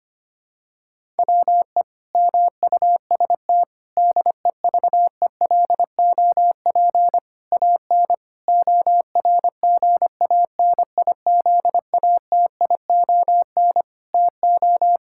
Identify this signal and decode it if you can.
Morse code, heard as a keyed tone, about 25 wpm, 700 Hz. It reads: WE MUST DEVELOP AN ORGANIZATION TO
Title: Sentences from Top 1000 Words - Review Part 5 - 25wpm